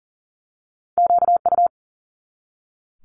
Morse code, heard as a keyed tone, about 41 wpm, 690 Hz. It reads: QU
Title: Multi-letter Phonemes 40wpm